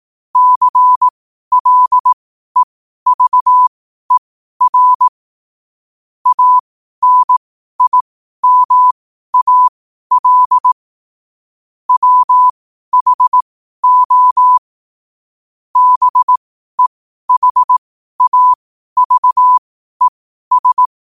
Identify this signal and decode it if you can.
Morse code, heard as a keyed tone, about 18 wpm, spaced as for 12 wpm, 1000 Hz. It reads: CLEVER ANIMAL WHO BEHAVES
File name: qotdfiles / 12wpm